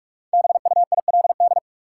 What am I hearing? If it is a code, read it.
Morse code, heard as a keyed tone, about 46 wpm, 700 Hz. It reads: BUILD
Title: Sentences from Top 1000 Words - Review Part 1 - New Word Drill - 45wpm